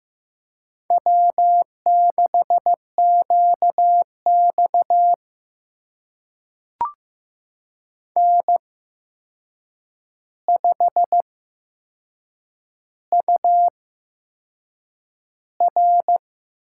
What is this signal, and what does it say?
Morse code, heard as a keyed tone, about 15 wpm, spaced as for 6 wpm, 700 Hz. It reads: W6QX N5UR
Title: USA Call Signs - 4 Characters - 8x Character Spacing - 15wpm